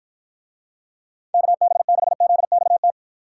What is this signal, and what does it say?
Morse code, heard as a keyed tone, about 52 wpm, 700 Hz. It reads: KB6ZXT